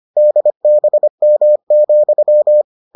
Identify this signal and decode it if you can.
Morse code, heard as a keyed tone, about 25 wpm, 600 Hz. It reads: DBM,